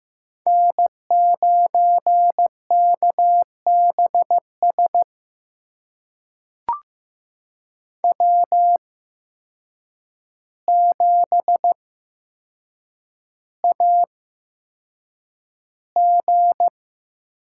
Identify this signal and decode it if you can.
Morse code, heard as a keyed tone, about 15 wpm, spaced as for 7 wpm, 700 Hz. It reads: N9KBS W7AG
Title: USA Call Signs - 5 Characters - 8x Character Spacing - 15wpm